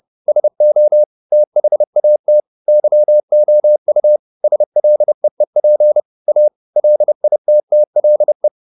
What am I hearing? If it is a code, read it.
Morse code, heard as a keyed tone, about 30 wpm, 600 Hz. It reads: SO THAT YOU SLEEP A LITTLE